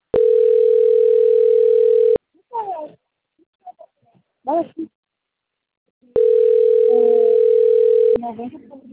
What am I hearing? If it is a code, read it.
Ringback tone (North America), 440 + 480 Hz, with speech.
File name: out- - - - - .1346